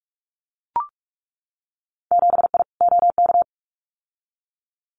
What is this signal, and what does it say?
Morse code, heard as a keyed tone, about 44 wpm, 700 Hz. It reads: 7I OK
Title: Winter Field Day Exchanges Encore 45wpm